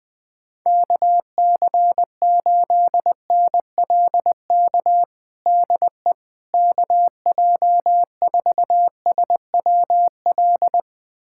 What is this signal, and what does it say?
Morse code, heard as a keyed tone, about 20 wpm, 700 Hz. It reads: KC8NLK DE KJ4SWL
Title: USA Callsign DE Callsign 20wpm